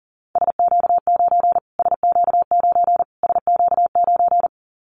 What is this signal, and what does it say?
Morse code, heard as a keyed tone, about 40 wpm, 700 Hz. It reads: SQ9 SQ9 SQ9
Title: Callsign Prefixes for Top 25 DXCC - Repeated 3x - 40wpm